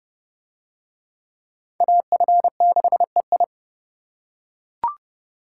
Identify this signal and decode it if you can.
Morse code, heard as a keyed tone, about 28 wpm, 700 Hz. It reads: AF6EI